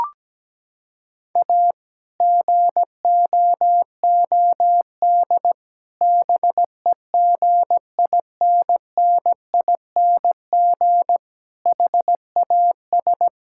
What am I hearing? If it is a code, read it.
Morse code, heard as a keyed tone, about 17 wpm, 700 Hz. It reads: A GOOD BEGINNING HAS
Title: Sentences from Top 1000 Words - Review Part 5 - 17wpm